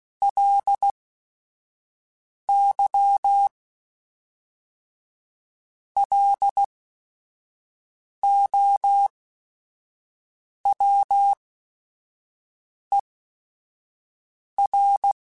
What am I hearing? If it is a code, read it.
Morse code, heard as a keyed tone, about 16 wpm, spaced as for 5 wpm, 800 Hz. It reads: LY LOWER
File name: Technical Subjects / Technical Subjects-mp3s / XC @5